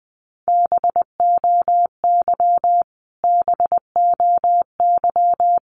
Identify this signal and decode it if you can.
Morse code, heard as a keyed tone, about 20 wpm, 700 Hz. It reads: BOY BOY
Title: Top 500 Words Repeated 20wpm